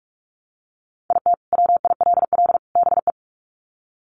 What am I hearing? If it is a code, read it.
Morse code, heard as a keyed tone, about 44 wpm, 700 Hz. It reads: IT WILL BE
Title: Sets of 3 Words 45wpm